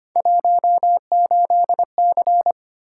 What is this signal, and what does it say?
Morse code, heard as a keyed tone, about 25 wpm, 700 Hz. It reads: 18C
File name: Three Letter-Number Combinations - 25wpm